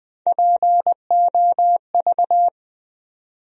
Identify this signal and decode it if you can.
Morse code, heard as a keyed tone, about 20 wpm, 700 Hz. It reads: POV
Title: USA Call Sign Postfixes - 3 Letters 20wpm